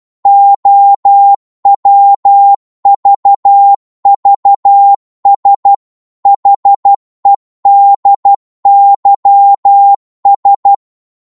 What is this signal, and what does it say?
Morse code, heard as a keyed tone, about 12 wpm, 800 Hz. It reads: OWVVS HEDYS
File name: Morse / ltr-12wpm